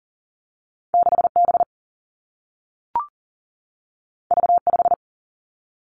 Morse code, heard as a keyed tone, about 38 wpm, 700 Hz. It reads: 6 B V 5
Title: Two Challenging Letter-Number Combinations 40wpm